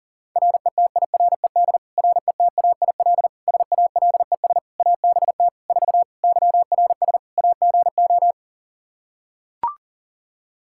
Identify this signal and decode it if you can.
Morse code, heard as a keyed tone, about 40 wpm, 700 Hz. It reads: RETIRED RETAIL SALES ABT 4 YRS AGO
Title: Practice QSO - Exchange 3 - More Info - 40wpm